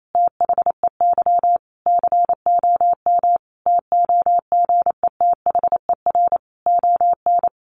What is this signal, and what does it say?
Morse code, heard as a keyed tone, about 28 wpm, 700 Hz. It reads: THEY COM TOGETHER ON
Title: Sentences from Top 1300 Words 28wpm